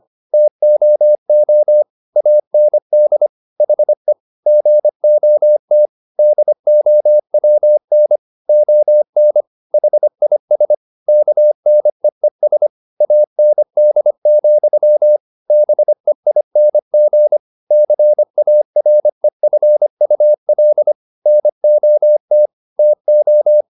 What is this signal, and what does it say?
Morse code, heard as a keyed tone, about 25 wpm, 600 Hz. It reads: TOO AND HE GOT DOWN ON HIS KNEES AND, BEING CAREFUL NOT TO
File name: the old man and the sea part 1 25wpm